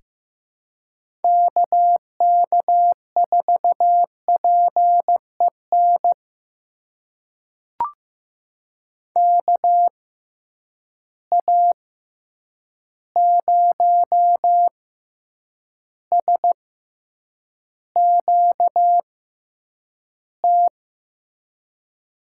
Morse code, heard as a keyed tone, about 15 wpm, spaced as for 8 wpm, 700 Hz. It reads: KK4PEN KA0SQT
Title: USA Call Signs - 6 Characters - 6x Character Spacing - 15wpm